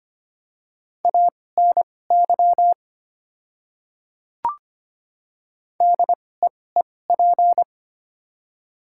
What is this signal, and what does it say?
Morse code, heard as a keyed tone, about 25 wpm, spaced as for 18 wpm, 700 Hz. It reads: ANY DEEP
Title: 2x Character Spacing - Top 500 Words - 25wpm